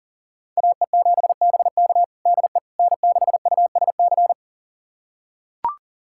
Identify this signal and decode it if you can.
Morse code, heard as a keyed tone, about 40 wpm, 700 Hz. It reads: AE7BK DE N6USC
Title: USA Callsign DE Callsign - 5 Characters - 40wpm